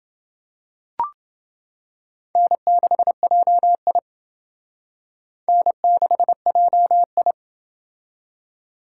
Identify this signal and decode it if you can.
Morse code, heard as a keyed tone, about 28 wpm, 700 Hz. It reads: N6JI N6JI